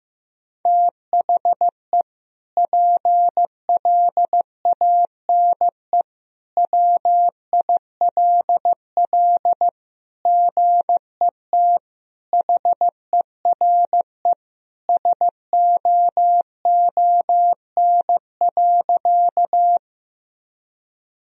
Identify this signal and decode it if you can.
Morse code, heard as a keyed tone, about 16 wpm, 700 Hz. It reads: THE PLANE WILL GET HERE SOON.
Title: Sentences from Top 1000 Words - Review Part 3 - 15wpm